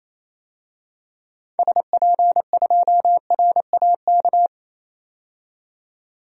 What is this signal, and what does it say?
Morse code, heard as a keyed tone, about 28 wpm, 700 Hz. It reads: SP2RAK
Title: DXCC-100 Call Signs - 6 Characters - 28wpm